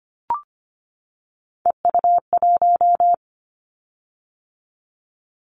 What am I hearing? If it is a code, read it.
Morse code, heard as a keyed tone, about 25 wpm, 700 Hz. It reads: EU1
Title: Callsign Prefixes for Top 100 DXCC - 25wpm